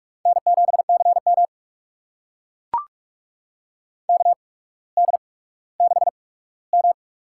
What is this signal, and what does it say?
Morse code, heard as a keyed tone, about 42 wpm, spaced as for 19 wpm, 700 Hz. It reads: N7CM KD6M
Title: USA Call Signs - 4 Characters - 8x Character Spacing - 45wpm